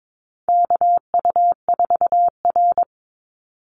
Morse code, heard as a keyed tone, about 22 wpm, 700 Hz. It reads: KU4R
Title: NA SOTA Chaser Callsigns 22wpm